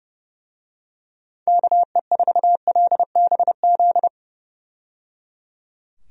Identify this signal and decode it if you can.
Morse code, heard as a keyed tone, about 30 wpm, 700 Hz. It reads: KE4LBZ
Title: USA Call Signs - 6 Characters - 30wpm